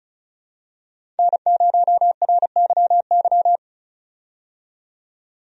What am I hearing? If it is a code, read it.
Morse code, heard as a keyed tone, about 36 wpm, 700 Hz. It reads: N0RYY
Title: USA Call Signs - 5 Characters - 35wpm